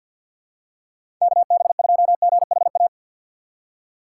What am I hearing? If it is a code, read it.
Morse code, heard as a keyed tone, about 47 wpm, 700 Hz. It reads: KB2GHA
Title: USA Call Signs - 6 Characters - 8x Character Spacing - 50wpm